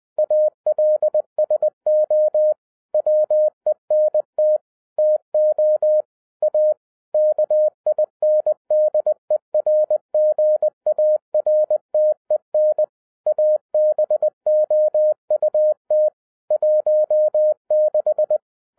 Morse code, heard as a keyed tone, about 20 wpm, 610 Hz. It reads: ALSO WENT TO A KINDERGARTEN ABOUT 16
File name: 2011-10-mp3 / n03-TARO JR0QWW 20wpm